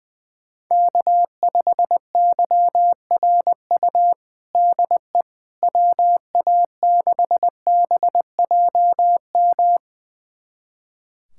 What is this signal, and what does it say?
Morse code, heard as a keyed tone, about 20 wpm, 700 Hz. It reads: K5YRU DE WA6BJM
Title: USA Callsign DE Callsign 20wpm